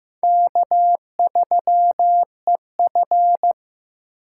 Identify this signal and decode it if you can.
Morse code, heard as a keyed tone, about 18 wpm, 700 Hz. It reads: K3EF
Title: Speed Racing - USA 1x2 Call Signs 15wpm